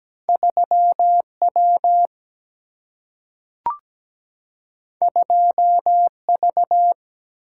Morse code, heard as a keyed tone, about 17 wpm, 700 Hz. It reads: 3W 2V
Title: Single-Letter Single-Number 17wpm